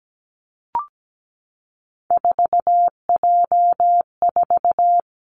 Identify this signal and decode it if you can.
Morse code, heard as a keyed tone, about 17 wpm, 700 Hz. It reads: 4J4
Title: Three Challenging Letter-Number Combinations 17wpm